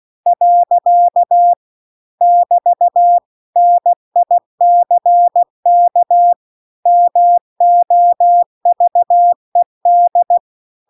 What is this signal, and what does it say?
Morse code, heard as a keyed tone, about 16 wpm, 700 Hz. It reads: . <BT> NICK MOVED